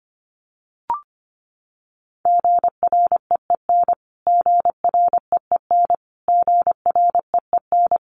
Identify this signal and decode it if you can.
Morse code, heard as a keyed tone, about 24 wpm, 700 Hz. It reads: GREEN GREEN GREEN
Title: Top 500 Words Repeated 25wpm